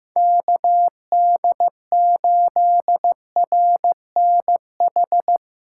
Morse code, heard as a keyed tone, about 15 wpm, 700 Hz. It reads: KD8RNH
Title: USA Callsign DE Callsign 15wpm